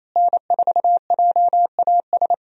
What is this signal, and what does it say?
Morse code, heard as a keyed tone, about 27 wpm, 700 Hz. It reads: N4JAS
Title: USA Call Signs - 5 Characters - Repeated 3x - 28wpm